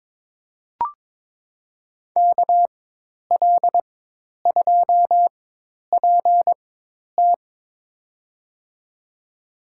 Morse code, heard as a keyed tone, about 22 wpm, spaced as for 10 wpm, 700 Hz. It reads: KL2PT
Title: USA Call Signs - 5 Characters - 4x Character Spacing - 22wpm